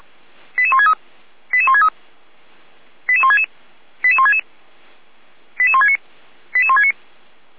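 Five-tone selective call (ZVEI), heard as quick runs of five tones, tones of about 70 ms. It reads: 88163 88163 88166 88166 88169 88169
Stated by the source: ZVEI five-tone selective call received by SDR.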